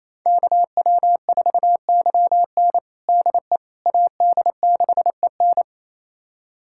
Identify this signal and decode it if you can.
Morse code, heard as a keyed tone, about 28 wpm, 700 Hz. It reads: KW4YN DE AD6EN
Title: USA Callsign DE Callsign - 5 Characters - 28wpm